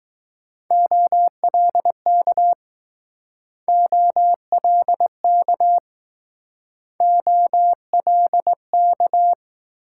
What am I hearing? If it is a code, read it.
Morse code, heard as a keyed tone, about 21 wpm, 700 Hz. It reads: OLK OLK OLK